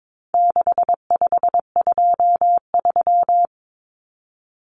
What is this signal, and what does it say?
Morse code, heard as a keyed tone, about 22 wpm, 700 Hz. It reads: 6523
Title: Numbers - 4-Digit 22wpm